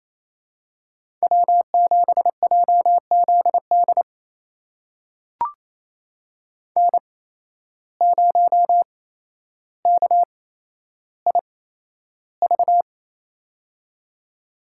Morse code, heard as a keyed tone, about 27 wpm, spaced as for 12 wpm, 700 Hz. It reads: W7JZD N0KIV
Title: USA Call Signs - 5 Characters - 8x Character Spacing - 28wpm